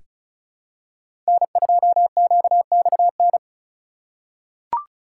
Morse code, heard as a keyed tone, about 34 wpm, 700 Hz. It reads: N2QXN
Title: USA Call Signs - 5 Characters - 6x Character Spacing - 35wpm